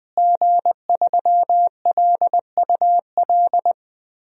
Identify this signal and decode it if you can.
Morse code, heard as a keyed tone, about 20 wpm, 700 Hz. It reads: G3LUL
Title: DXCC-100 Call Signs - 5 Characters - Repeated 3x - 20wpm